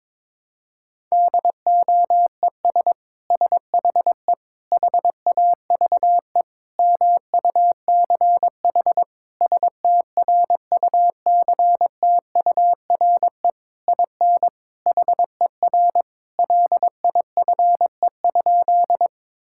Morse code, heard as a keyed tone, about 22 wpm, 700 Hz. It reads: DOES SHE HAVE MUCH STRUCTURE IN HER LIFE?
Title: Sentences from Top 1000 Words - Review Part 6 - 22wpm